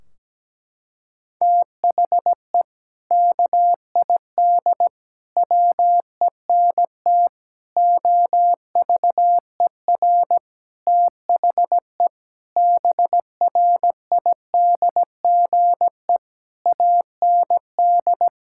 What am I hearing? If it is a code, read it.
Morse code, heard as a keyed tone, about 17 wpm, 700 Hz. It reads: THE KID WENT OVER THE BRIDGE AND
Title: Sentences from Top 1600 Words 17wpm